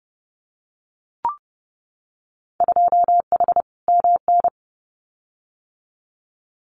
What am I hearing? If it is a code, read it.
Morse code, heard as a keyed tone, about 30 wpm, 700 Hz. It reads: 2H MN
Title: Winter Field Day Exchanges 30wpm